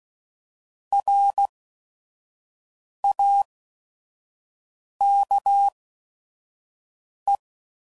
Morse code, heard as a keyed tone, about 16 wpm, spaced as for 5 wpm, 800 Hz. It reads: RAKE
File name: Prefixes / By First Letter / Rs